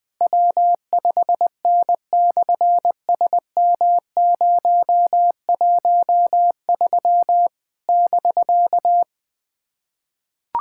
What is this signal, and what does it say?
Morse code, heard as a keyed tone, about 20 wpm, 700 Hz. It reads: W5N/SM013 <BK>
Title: SOTA QSO Practice 20wpm